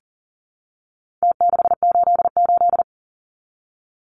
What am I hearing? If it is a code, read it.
Morse code, heard as a keyed tone, about 40 wpm, 700 Hz. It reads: T688
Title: Numbers - Contest Style - 4-Digit 40wpm